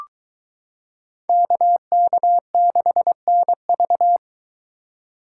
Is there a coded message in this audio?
Yes: Morse code from 1.3 to 4.2 s, heard as a keyed tone, about 23 wpm, 700 Hz. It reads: KK6NV